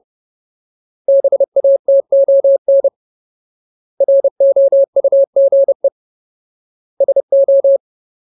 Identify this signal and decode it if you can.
Morse code, heard as a keyed tone, about 30 wpm, 550 Hz. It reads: BATON ROUGE SO
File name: Ultimate Word List / 30WPM Ultimate